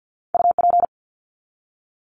Morse code, heard as a keyed tone, about 48 wpm, 700 Hz. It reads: UP
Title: Top 300 Words 50wpm